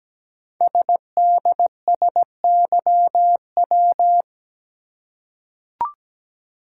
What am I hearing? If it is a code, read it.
Morse code, heard as a keyed tone, about 17 wpm, 700 Hz. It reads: SDSYW
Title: Code Groups - 5 Letter:Numbers - 17wpm